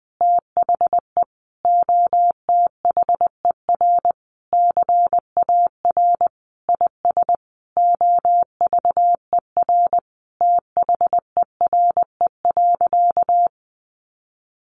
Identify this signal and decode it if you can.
Morse code, heard as a keyed tone, about 20 wpm, 700 Hz. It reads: THE OTHER CAR IS OVER THERE.
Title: Sentences from Top 1000 Words 20wpm